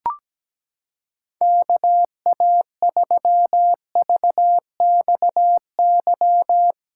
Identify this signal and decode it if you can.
Morse code, heard as a keyed tone, about 17 wpm, 700 Hz. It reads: KA3VXY
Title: USA Callsign DE Callsign 17wpm